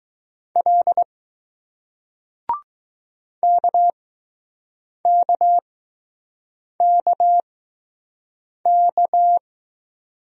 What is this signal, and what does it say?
Morse code, heard as a keyed tone, about 20 wpm, spaced as for 6 wpm, 700 Hz. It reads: L KKKK